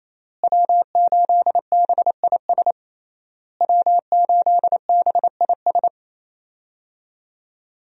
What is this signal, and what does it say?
Morse code, heard as a keyed tone, about 28 wpm, 700 Hz. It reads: W8BIS W8BIS
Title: USA Call Signs - 5 Characters - Repeated 3x - 28wpm